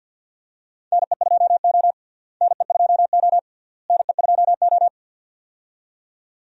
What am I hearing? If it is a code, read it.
Morse code, heard as a keyed tone, about 47 wpm, 700 Hz. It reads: NE2O NE2O NE2O
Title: USA Call Signs - 4 Characters - Repeated 3x - 50wpm